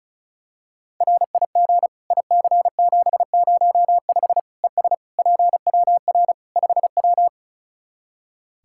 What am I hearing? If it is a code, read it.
Morse code, heard as a keyed tone, about 36 wpm, 700 Hz. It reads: RIG IC705 ES PWR 5W
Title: Practice QSO - Exchange 2 - Info - 35wpm